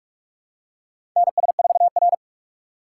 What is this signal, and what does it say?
Morse code, heard as a keyed tone, about 46 wpm, 700 Hz. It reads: NS4R
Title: USA Call Signs - 4 Characters - 4x Character Spacing - 45wpm